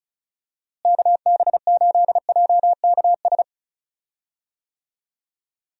Morse code, heard as a keyed tone, about 34 wpm, 700 Hz. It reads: KB8JKS
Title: USA Call Signs - 6 Characters - Repeated 2x - 35wpm